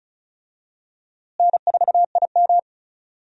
Morse code, heard as a keyed tone, about 36 wpm, 700 Hz. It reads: N4IM
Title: USA Call Signs - 4 Characters - Repeated 2x - 35wpm